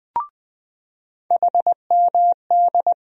Morse code, heard as a keyed tone, about 20 wpm, 700 Hz. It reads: HMD
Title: USA Call Sign Postfixes - 3 Letters 20wpm